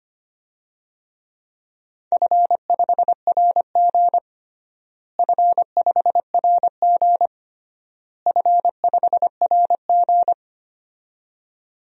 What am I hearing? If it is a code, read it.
Morse code, heard as a keyed tone, about 25 wpm, 700 Hz. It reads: F5RG F5RG F5RG
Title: DXCC-100 Call Signs - 4 Characters - Repeated 3x - 25wpm